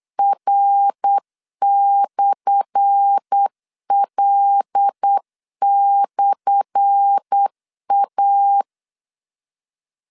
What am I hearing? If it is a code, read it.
Morse code, heard as a keyed tone, about 8 wpm, 800 Hz. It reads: R/L/A
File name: LEZIONI / 10 SIGNS / 10A SLASH 8wpm 3452